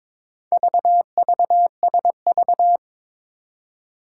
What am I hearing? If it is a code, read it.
Morse code, heard as a keyed tone, about 22 wpm, 700 Hz. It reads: VVSV
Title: Four Challenging Letter-Number Combinations 22wpm